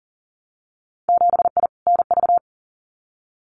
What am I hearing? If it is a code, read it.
Morse code, heard as a keyed tone, about 40 wpm, 700 Hz. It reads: 7I NV
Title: Winter Field Day Exchanges Encore 40wpm